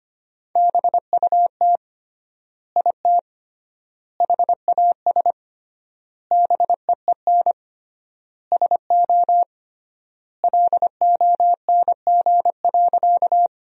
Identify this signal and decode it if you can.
Morse code, heard as a keyed tone, about 25 wpm, 700 Hz. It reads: BUT IT HAS BEEN SO LONG.
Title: 3x Word-Spacing - Sentences from Top 100 Words - 25wpm